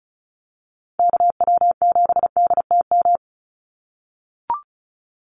Morse code, heard as a keyed tone, about 34 wpm, 700 Hz. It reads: KW7DTM
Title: USA Call Signs 35wpm